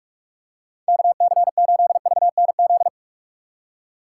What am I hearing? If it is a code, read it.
Morse code, heard as a keyed tone, about 46 wpm, 700 Hz. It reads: KC8VNZ
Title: USA Call Signs - 6 Characters - 2x Character Spacing - 45wpm